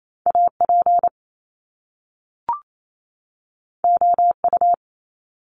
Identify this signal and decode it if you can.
Morse code, heard as a keyed tone, about 28 wpm, 700 Hz. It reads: AP OU
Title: Top Two Letter Combinations in Words 28wpm